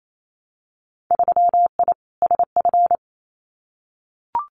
Morse code, heard as a keyed tone, about 28 wpm, 700 Hz. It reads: 3I SF
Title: Winter Field Day Exchanges 28wpm